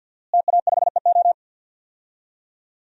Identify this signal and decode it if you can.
Morse code, heard as a keyed tone, about 52 wpm, 700 Hz. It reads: TA5EO
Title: DXCC-100 Call Signs - 5 Characters - Repeated 2x - 50wpm